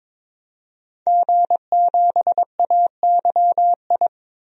Morse code, heard as a keyed tone, about 22 wpm, 700 Hz. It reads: G7AYI
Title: DXCC-100 Call Signs - 5 Characters - 22wpm